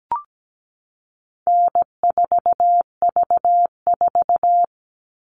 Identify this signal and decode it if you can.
Morse code, heard as a keyed tone, about 17 wpm, 700 Hz. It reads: N4V4
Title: All Cut Numbers - 4-Digit 17wpm